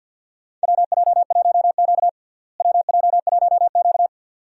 Morse code, heard as a keyed tone, about 50 wpm, 700 Hz. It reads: WJ1Q WJ1Q
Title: USA Call Signs - 4 Characters - Repeated 2x - 50wpm